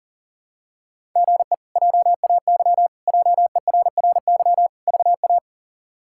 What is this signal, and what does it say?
Morse code, heard as a keyed tone, about 40 wpm, 700 Hz. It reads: GE JAY JERRY VA
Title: K1USN SST QSO Practice - 40wpm